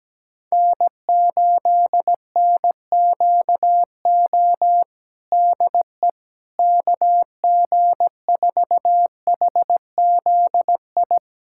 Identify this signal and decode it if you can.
Morse code, heard as a keyed tone, about 17 wpm, 700 Hz. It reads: N8NQO DE KG4HZI
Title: USA Callsign DE Callsign 17wpm